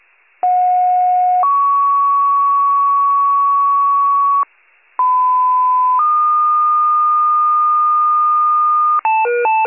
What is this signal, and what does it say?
Two-tone page, heard as a one-second tone, then a three-second tone, about 730 then 1120 Hz, about 1010 then 1220 Hz. It is heard from 0.4 to 9.0 s.
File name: wcalert